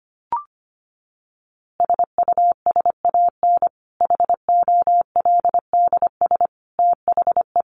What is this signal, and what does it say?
Morse code, heard as a keyed tone, about 25 wpm, 700 Hz. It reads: SUSAN HOLDS THE
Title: Easy Phrases 25wpm